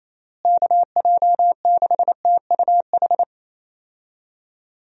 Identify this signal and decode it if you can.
Morse code, heard as a keyed tone, about 28 wpm, 700 Hz. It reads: KJ6TUH
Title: USA Call Signs - 6 Characters - Repeated 3x - 28wpm